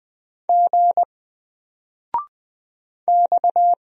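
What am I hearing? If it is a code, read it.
Morse code, heard as a keyed tone, about 20 wpm, 700 Hz. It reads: G X
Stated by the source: Single letter and number Morse drill at 20 wpm, from N4LCD.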